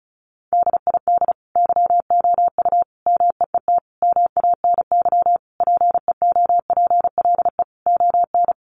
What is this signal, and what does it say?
Morse code, heard as a keyed tone, about 34 wpm, 700 Hz. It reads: DID YOU MEET MANY PEOPLE ON
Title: Sentences from Top 1100 Words 35wpm